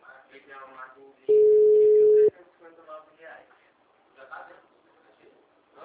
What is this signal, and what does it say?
Ringback tone (Europe, 425 Hz), with speech.